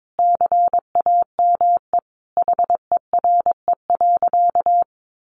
Morse code, heard as a keyed tone, about 22 wpm, 700 Hz. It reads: CAME HERE.
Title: Sentences from Top 500 Words 22wpm